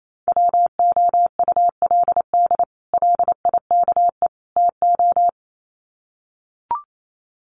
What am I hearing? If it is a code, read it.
Morse code, heard as a keyed tone, about 28 wpm, 700 Hz. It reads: WOULD LIKE TO
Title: Sets of 3 Words 28wpm